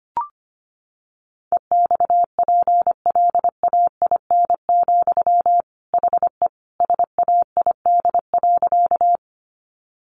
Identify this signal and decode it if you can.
Morse code, heard as a keyed tone, about 25 wpm, 700 Hz. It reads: EXPLAIN, HE SAID.
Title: Sentences from Top 1100 Words 25wpm